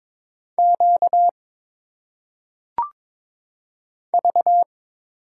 Morse code, heard as a keyed tone, about 22 wpm, 700 Hz. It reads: Q V
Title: Single Letters - Phonetic Pronunciation 22wpm